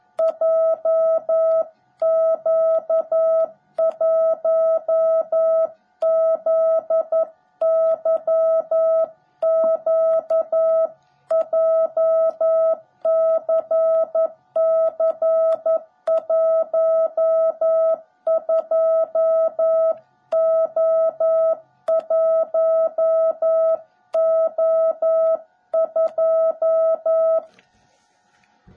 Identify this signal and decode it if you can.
Morse code, heard as a keyed tone, about 11 wpm, 660 Hz. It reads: JQ1ZYQ JCC12O1O2
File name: jh8cbh / 2021cw / cw17b